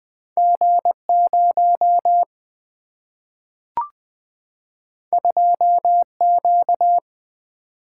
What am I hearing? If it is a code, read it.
Morse code, heard as a keyed tone, about 20 wpm, 700 Hz. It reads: G 0 2 Q
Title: Single-Letter Single-Number 20wpm